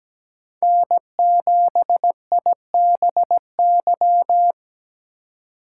Morse code, heard as a keyed tone, about 20 wpm, 700 Hz. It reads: N7IBY
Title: Speed Racing - USA 1x3 Call Signs 17wpm